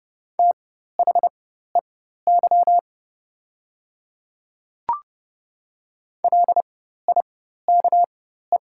Morse code, heard as a keyed tone, about 30 wpm, spaced as for 14 wpm, 700 Hz. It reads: THEY LIKE
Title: 4x Character Spacing - Top 100 Words - 30wpm